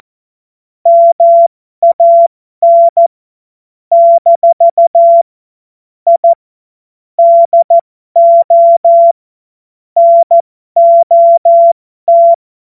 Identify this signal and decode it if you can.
Morse code, heard as a keyed tone, about 14 wpm, spaced as for 12 wpm, 680 Hz. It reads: MAN - I DO NOT
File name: Morsekurs / Witze